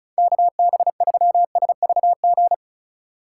Morse code, heard as a keyed tone, about 45 wpm, 700 Hz. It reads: KB3SVG